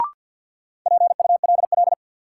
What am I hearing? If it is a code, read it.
Morse code, heard as a keyed tone, about 53 wpm, 700 Hz. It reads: PULL
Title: Sentences from Top 400 Words - New Word Drill - 50wpm